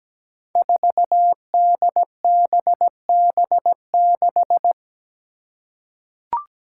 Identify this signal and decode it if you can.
Morse code, heard as a keyed tone, about 17 wpm, 700 Hz. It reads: 4DBB6
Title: Five Challenging Letter-Number Combinations 17wpm